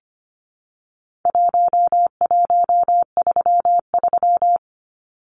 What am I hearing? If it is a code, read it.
Morse code, heard as a keyed tone, about 25 wpm, 700 Hz. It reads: 1133
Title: Numbers - 4-Digit 25wpm